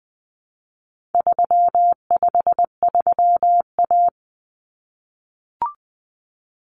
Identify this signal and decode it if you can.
Morse code, heard as a keyed tone, about 20 wpm, 700 Hz. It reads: 353A
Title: Common Cut Numbers - 4-Digit 20wpm